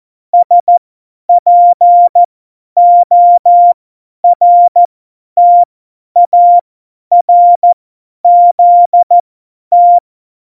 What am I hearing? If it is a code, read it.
Morse code, heard as a keyed tone, about 14 wpm, spaced as for 10 wpm, 700 Hz. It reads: SPORTARZT
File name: Morsekurs / Witze / Witz 13. Februar 2018, 50 F70